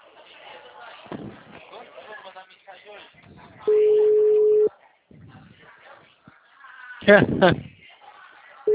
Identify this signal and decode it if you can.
Ringback tone (Europe, 425 Hz), with speech.